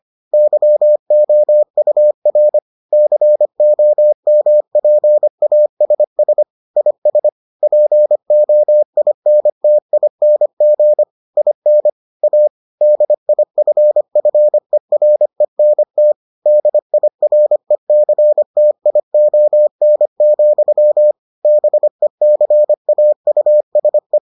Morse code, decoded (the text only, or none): YOUR COMPASS IS POINTING IN A DIFFERENT DIRECTION, BECAUSE